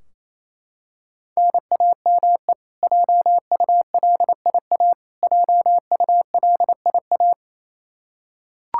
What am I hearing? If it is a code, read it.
Morse code, heard as a keyed tone, about 28 wpm, 700 Hz. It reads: NAME JULIA JULIA
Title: Practice QSO - Exchange 1 - Beginning - 28wpm